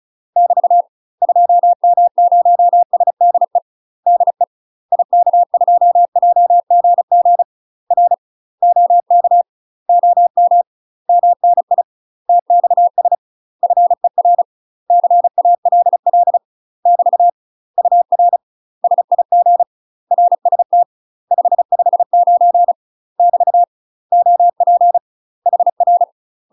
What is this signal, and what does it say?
Morse code, heard as a keyed tone, about 36 wpm, 700 Hz. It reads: <BT> 2M0SDE DE IK2JGG R OK OM MNI TXS FER CALL <BT> UR SIG RST 559 <BT> OP HR
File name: UK MASTER QSOs TRAINING / 35WPM / UK MASTER QSOs 0038